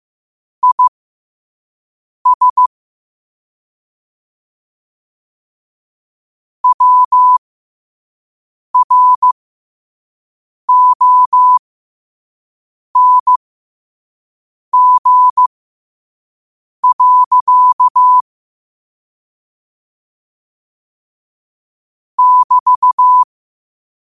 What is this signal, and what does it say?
Morse code, heard as a keyed tone, about 15 wpm, spaced as for 5 wpm, 1000 Hz. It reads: IS WRONG. <BT>